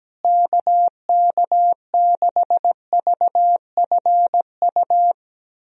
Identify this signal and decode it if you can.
Morse code, heard as a keyed tone, about 17 wpm, 700 Hz. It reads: KK6VFU
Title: USA Call Signs - 6 Characters - Repeated 3x - 17wpm